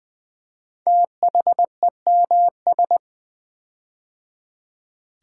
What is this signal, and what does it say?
Morse code, heard as a keyed tone, about 20 wpm, 700 Hz. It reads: THEMS
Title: Build A Word - 5 Letters At A Time - 10-Letter Words - 20wpm